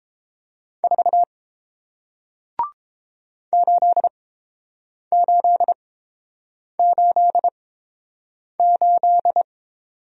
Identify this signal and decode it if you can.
Morse code, heard as a keyed tone, about 31 wpm, spaced as for 7 wpm, 700 Hz. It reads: 4 8888